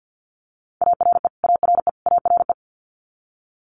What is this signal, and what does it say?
Morse code, heard as a keyed tone, about 50 wpm, 690 Hz. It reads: ARE ARE ARE
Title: Multi-letter Phonemes Repeated 3x 50wpm